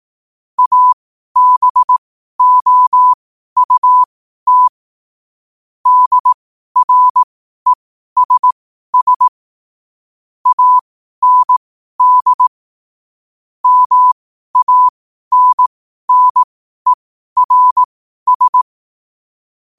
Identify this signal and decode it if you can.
Morse code, heard as a keyed tone, about 18 wpm, spaced as for 12 wpm, 1000 Hz. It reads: ABOUT DRESS AND MANNERS
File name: qotdfiles / 12wpm